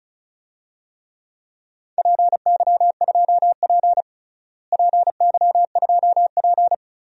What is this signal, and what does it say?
Morse code, heard as a keyed tone, about 36 wpm, 700 Hz. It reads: PY2P PY2P
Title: DXCC-100 Call Signs - 4 Characters - Repeated 3x - 35wpm